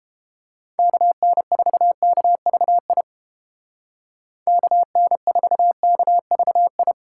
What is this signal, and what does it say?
Morse code, heard as a keyed tone, about 29 wpm, 700 Hz. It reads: KN4KVI KN4KVI